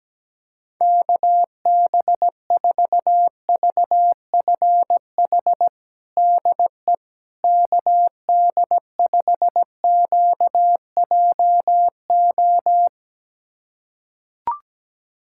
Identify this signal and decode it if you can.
Morse code, heard as a keyed tone, about 17 wpm, 700 Hz. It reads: KB4VFH DE KD5QJO
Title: USA Callsign DE Callsign 17wpm